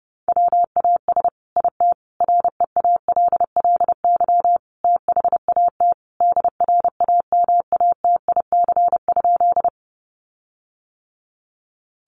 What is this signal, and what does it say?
Morse code, heard as a keyed tone, about 30 wpm, 700 Hz. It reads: WAS IT REALLY THAT DRAMATIC?
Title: Sentences from Top 1300 Words 30wpm